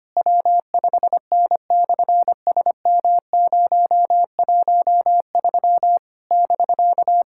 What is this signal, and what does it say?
Morse code, heard as a keyed tone, about 25 wpm, 700 Hz. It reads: W5N/SM013 <BK>
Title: SOTA QSO Practice 25wpm